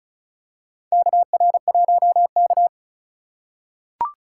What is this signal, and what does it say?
Morse code, heard as a keyed tone, about 36 wpm, 700 Hz. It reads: KR1K
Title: USA Call Signs - 4 Characters - 4x Character Spacing - 35wpm